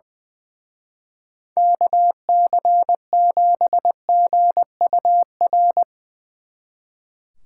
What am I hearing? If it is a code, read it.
Morse code, heard as a keyed tone, about 20 wpm, 700 Hz. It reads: KC7GUR
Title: Speed Racing - USA 2x3 Call Signs 20wpm